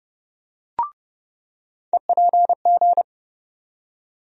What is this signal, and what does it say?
Morse code, heard as a keyed tone, about 30 wpm, 700 Hz. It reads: EPG
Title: USA Call Sign Postfixes - 3 Letters 30wpm